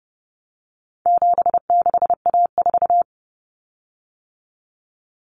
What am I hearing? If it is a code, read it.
Morse code, heard as a keyed tone, about 30 wpm, 700 Hz. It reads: 76A4
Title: Common Cut Numbers - 4-Digit 30wpm